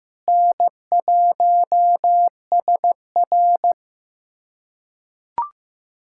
Morse code, heard as a keyed tone, about 15 wpm, 700 Hz. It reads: N1SR
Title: USA Callsign DE Callsign - 4 Characters - 15wpm